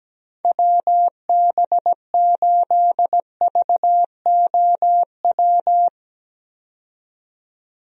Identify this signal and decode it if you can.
Morse code, heard as a keyed tone, about 17 wpm, 700 Hz. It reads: WB8VOW
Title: USA Call Signs - 6 Characters - 17wpm